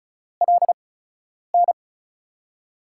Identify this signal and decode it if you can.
Morse code, heard as a keyed tone, about 35 wpm, spaced as for 10 wpm, 700 Hz. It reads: LN
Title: USA Call Signs - 5 Characters - 8x Character Spacing - 35wpm